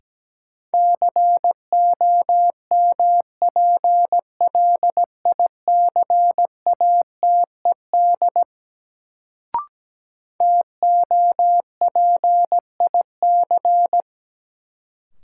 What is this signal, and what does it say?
Morse code, heard as a keyed tone, about 17 wpm, 700 Hz. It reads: COMPLICATED TOPIC
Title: Sentences from Top 1200 Words - New Word Drill Alt - 17wpm